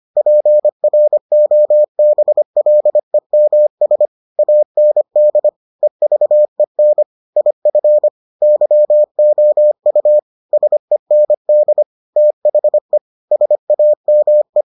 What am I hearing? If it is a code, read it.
Morse code, heard as a keyed tone, about 25 wpm, 600 Hz. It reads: PROBLEMS AND EVEN IF YOU SEND THE SAME